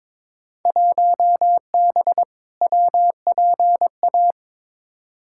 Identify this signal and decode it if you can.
Morse code, heard as a keyed tone, about 22 wpm, 700 Hz. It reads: 1B WPA
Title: Field Day Exchanges Encore - 22wpm